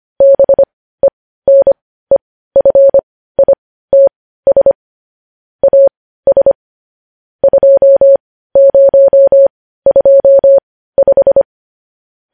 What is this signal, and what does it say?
Morse code, heard as a keyed tone, about 25 wpm, spaced as for 15 wpm, 550 Hz. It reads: BENEFITS AS 2025
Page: News Headlines in Morse Code at 15 WPM